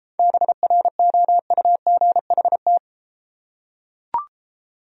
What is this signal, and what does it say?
Morse code, heard as a keyed tone, about 34 wpm, 700 Hz. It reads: BROUGHT